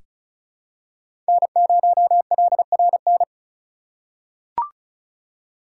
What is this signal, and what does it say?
Morse code, heard as a keyed tone, about 36 wpm, 700 Hz. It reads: N0LRN
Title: USA Call Signs - 8x Character Spacing - 35wpm